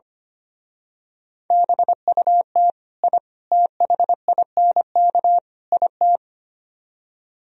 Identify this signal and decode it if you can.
Morse code, heard as a keyed tone, about 28 wpm, 700 Hz. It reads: BUT I THINK IT
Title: Speed-Racing - Sets of 4 Words 25wpm